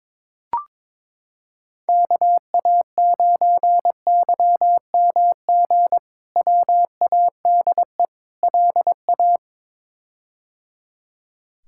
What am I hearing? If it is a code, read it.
Morse code, heard as a keyed tone, about 21 wpm, 700 Hz. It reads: KA9YMG WADE LA
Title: K1USN SST QSO Practice - 22wpm